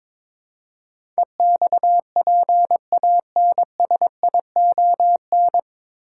Morse code, heard as a keyed tone, about 21 wpm, 700 Hz. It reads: EXPANSION
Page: Morse Code Ninja practice 22wpm